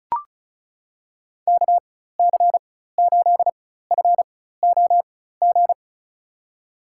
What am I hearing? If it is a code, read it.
Morse code, heard as a keyed tone, about 35 wpm, spaced as for 16 wpm, 700 Hz. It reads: KC8FOG
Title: USA Call Signs - 4x Character Spacing - 35wpm